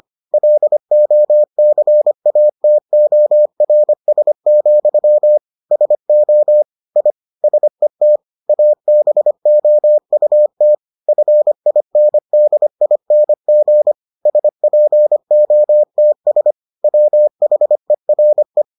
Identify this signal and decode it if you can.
Morse code, heard as a keyed tone, about 25 wpm, 600 Hz. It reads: LOCATORS, SO I SET ABOUT FINDING SPOTS WHERE